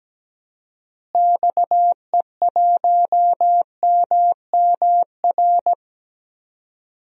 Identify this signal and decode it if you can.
Morse code, heard as a keyed tone, about 17 wpm, 700 Hz. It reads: XE1MMR
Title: DXCC-100 Call Signs - 6 Characters - Repeated 3x - 17wpm